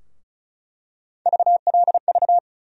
Morse code, heard as a keyed tone, about 36 wpm, 700 Hz. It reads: VLV
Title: USA Call Sign Postfixes - 3 Letters 35wpm